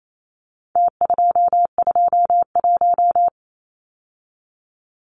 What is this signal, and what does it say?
Morse code, heard as a keyed tone, about 28 wpm, 700 Hz. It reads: T221
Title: Numbers - Contest Style - 4-Digit 28wpm